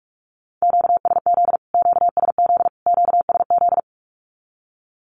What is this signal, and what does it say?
Morse code, heard as a keyed tone, about 44 wpm, 700 Hz. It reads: QSZ QSZ QSZ
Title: Q-Codes Repeated 3x 45wpm